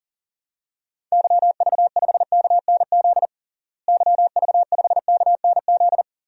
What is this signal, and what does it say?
Morse code, heard as a keyed tone, about 37 wpm, 700 Hz. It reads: YV5KNZ YV5KNZ
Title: DXCC-100 Call Signs - 6 Characters - Repeated 3x - 40wpm